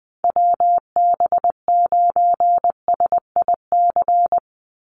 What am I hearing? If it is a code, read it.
Morse code, heard as a keyed tone, about 20 wpm, 700 Hz. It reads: WB9SIC
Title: USA Call Signs 20wpm